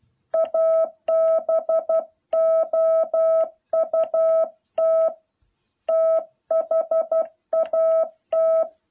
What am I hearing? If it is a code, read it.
Morse code, heard as a keyed tone, about 12 wpm, 660 Hz. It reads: ABOUT THAT
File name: jh8cbh / 2021cw